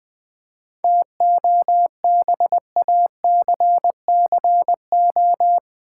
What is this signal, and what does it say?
Morse code, heard as a keyed tone, about 20 wpm, 700 Hz. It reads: TOBACCO
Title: 7 Letter Words - First 3 Letters Given - 20wpm